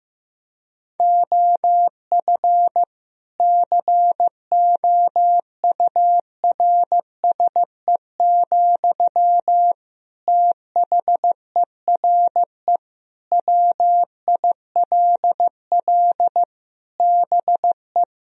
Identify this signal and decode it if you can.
Morse code, heard as a keyed tone, about 15 wpm, 700 Hz. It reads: OF COURSE, THERE WILL BE
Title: Sentences from Top 1000 Words - Review Part 4 - 15wpm